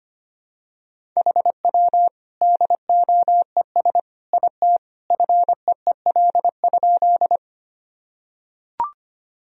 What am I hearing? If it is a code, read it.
Morse code, heard as a keyed tone, about 25 wpm, 700 Hz. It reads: HW DOES IT FEEL?
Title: Sentences from Top 400 Words and Abbreviations 25wpm